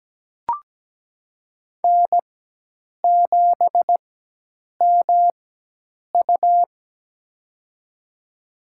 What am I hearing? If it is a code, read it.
Morse code, heard as a keyed tone, about 17 wpm, spaced as for 8 wpm, 700 Hz. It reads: N7MU
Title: USA Call Signs - 4 Characters - 4x Character Spacing - 17wpm